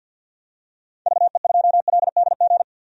Morse code, heard as a keyed tone, about 49 wpm, 700 Hz. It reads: VE2LDG
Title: DXCC-100 Call Signs - 6 Characters - 50wpm